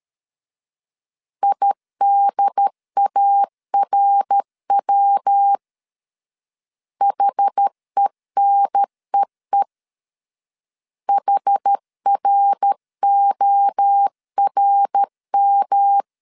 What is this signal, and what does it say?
Morse code, heard as a keyed tone, about 13 wpm, 800 Hz. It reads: IDARW HENEE HRORM
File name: CW / DET / 1 3 1500C S A 12wpm 2905